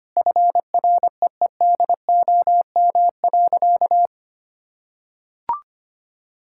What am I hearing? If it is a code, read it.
Morse code, heard as a keyed tone, about 25 wpm, 700 Hz. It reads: FREEDOM.
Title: Sentences from Top 1000 Words - Review Part 5 - 25wpm